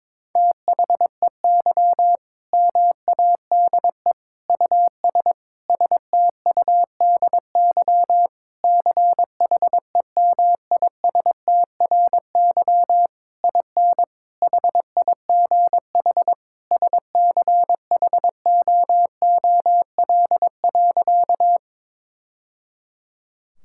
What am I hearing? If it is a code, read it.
Morse code, heard as a keyed tone, about 22 wpm, 700 Hz. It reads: THEY MADE US STUDY CHEMISTRY IN HIGH SCHOOL.
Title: Sentences from Top 1000 Words - Review Part 6 - 22wpm